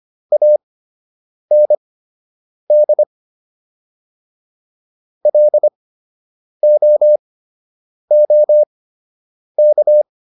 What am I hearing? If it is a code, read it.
Morse code, heard as a keyed tone, about 25 wpm, spaced as for 8 wpm, 600 Hz. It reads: AND LOOK